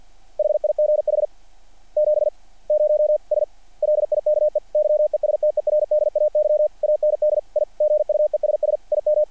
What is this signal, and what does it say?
Morse code, heard as a keyed tone, about 50 wpm, 600 Hz. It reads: HIGH 70S LIKE YESTERDAY AND I GUESS IM
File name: archived / CRSnet